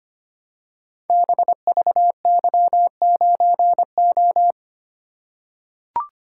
Morse code, heard as a keyed tone, about 25 wpm, 700 Hz. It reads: BVY9O
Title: Code Groups - 5 Letter:Numbers - 25wpm